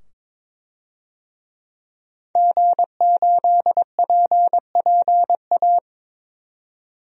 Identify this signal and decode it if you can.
Morse code, heard as a keyed tone, about 22 wpm, 700 Hz. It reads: G8PPA
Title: DXCC-100 Call Signs - 5 Characters - 22wpm